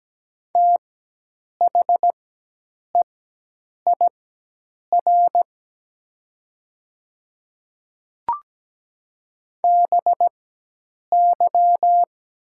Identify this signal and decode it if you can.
Morse code, heard as a keyed tone, about 17 wpm, spaced as for 8 wpm, 700 Hz. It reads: THEIR BY
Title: 4x Character Spacing - Top 200 Words - 17wpm